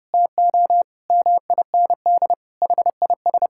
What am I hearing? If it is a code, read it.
Morse code, heard as a keyed tone, about 32 wpm, 700 Hz. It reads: TO MIND HIS
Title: Sentences from Top 1000 Words - Review Part 6 - 30wpm